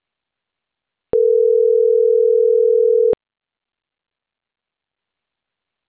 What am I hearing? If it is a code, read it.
Ringback tone (North America), 440 + 480 Hz.